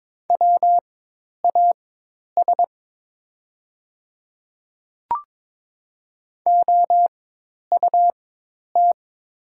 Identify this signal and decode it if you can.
Morse code, heard as a keyed tone, about 22 wpm, spaced as for 10 wpm, 700 Hz. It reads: WAS OUT
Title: 4x Character Spacing - Top 100 Words - 22wpm